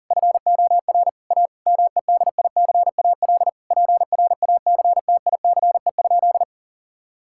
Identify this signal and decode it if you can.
Morse code, heard as a keyed tone, about 40 wpm, 700 Hz. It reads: FOR A MEDICAL PRACTICE?
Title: Sentences from Top 1000 Words - Review Part 5 - 40wpm